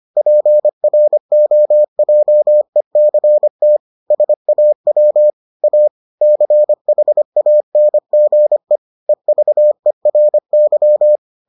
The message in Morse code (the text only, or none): PROJECT SAW A CHANGE EVERY